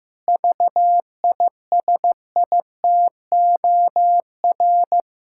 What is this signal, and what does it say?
Morse code, heard as a keyed tone, about 15 wpm, 700 Hz. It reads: VISITOR
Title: Pangram Challenge v2 - 15wpm